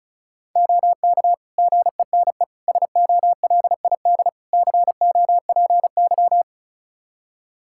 Morse code, heard as a keyed tone, about 36 wpm, 700 Hz. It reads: OK GENE SOLID COPY
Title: Practice QSO - Exchange 3 - More Info - 35wpm